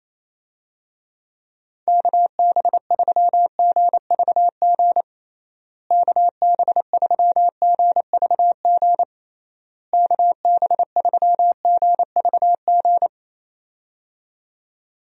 Morse code, heard as a keyed tone, about 28 wpm, 700 Hz. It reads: KB3GVG KB3GVG KB3GVG
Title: USA Call Signs - 6 Characters - Repeated 3x - 28wpm